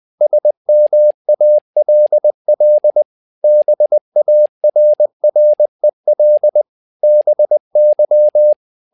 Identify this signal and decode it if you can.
Morse code, heard as a keyed tone, about 20 wpm, 600 Hz. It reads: SMALL BARREL BY
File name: learning center / tower trouble / tower trouble 20wpm